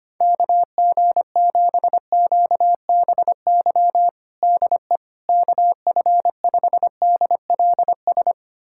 Morse code, heard as a keyed tone, about 25 wpm, 700 Hz. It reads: KG7QBY DE KF5DLS
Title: USA Callsign DE Callsign - 6 Characters - 25wpm